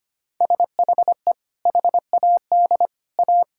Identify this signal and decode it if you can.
Morse code, heard as a keyed tone, about 25 wpm, 700 Hz. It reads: SHE HAD A
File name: practice / 25wpm / Sentences with 2-Word Phrases - 25wpm